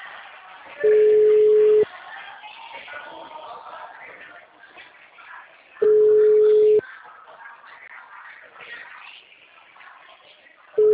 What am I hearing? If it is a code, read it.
Ringback tone (Europe, 425 Hz), with speech.